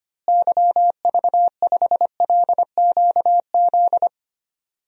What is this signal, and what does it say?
Morse code, heard as a keyed tone, about 25 wpm, 700 Hz. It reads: YV5LQZ
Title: DXCC-100 Call Signs - 6 Characters - Repeated 2x - 25wpm